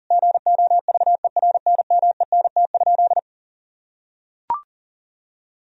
Morse code, heard as a keyed tone, about 37 wpm, 700 Hz. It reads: GOVERNMENT?
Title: Sentences from Top 1000 Words - Review Part 3 - 40wpm